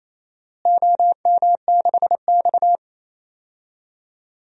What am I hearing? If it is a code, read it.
Morse code, heard as a keyed tone, about 28 wpm, 700 Hz. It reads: OM6X
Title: DXCC-100 Call Signs - 4 Characters - Repeated 2x - 28wpm